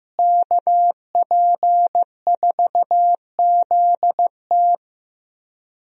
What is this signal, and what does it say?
Morse code, heard as a keyed tone, about 15 wpm, 700 Hz. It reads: KP4ZT
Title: USA Call Signs - 5 Characters - Repeated 2x - 15wpm